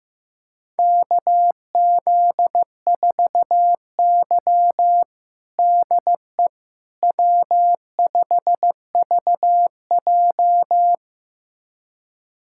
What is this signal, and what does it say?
Morse code, heard as a keyed tone, about 15 wpm, 700 Hz. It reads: KZ4Y DE W5VJ
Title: USA Callsign DE Callsign - 4 Characters - 15wpm